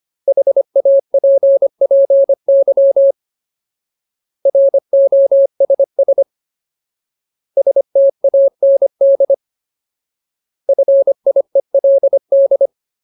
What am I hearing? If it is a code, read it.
Morse code, heard as a keyed tone, about 25 wpm, 550 Hz. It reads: HAPPY ROSS STAND FIELD
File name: Ultimate Word List / 25WPM Ultimate